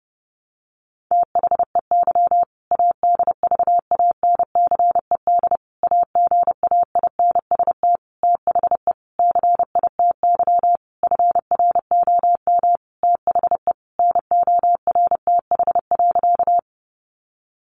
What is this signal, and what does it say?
Morse code, heard as a keyed tone, about 30 wpm, 700 Hz. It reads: THEY ADVANCED AGAINST THE CITY FROM THE NORTH.
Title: Sentences from Top 1200 Words 30wpm